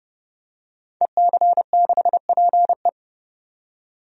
Morse code, heard as a keyed tone, about 32 wpm, 700 Hz. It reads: EC6PE
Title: DXCC-100 Call Signs - 5 Characters - 30wpm